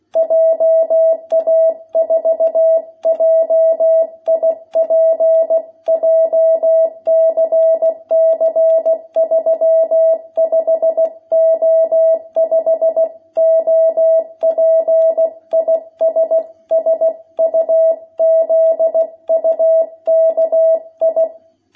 Morse code, heard as a keyed tone, about 16 wpm, 660 Hz. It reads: JA4JIP JCC 35O5 OP IS SUZUKI
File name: jh8cbh / 2021cw